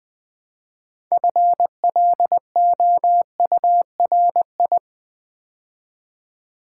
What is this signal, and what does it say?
Morse code, heard as a keyed tone, about 20 wpm, 700 Hz. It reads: FLOURI
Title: Build A Word - 3 Letters At A Time - 8-Letter Words - 20wpm